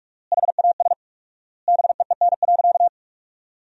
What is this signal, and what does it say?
Morse code, heard as a keyed tone, about 46 wpm, 700 Hz. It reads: HAS BEEN.
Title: 4x Word-Spacing - Sentences from Top 300 Words - 45wpm